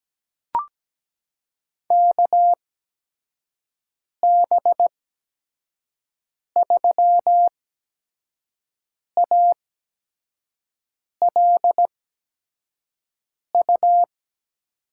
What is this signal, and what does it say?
Morse code, heard as a keyed tone, about 17 wpm, spaced as for 5 wpm, 700 Hz. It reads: KB3ALU
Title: USA Call Signs - 6 Characters - 8x Character Spacing - 17wpm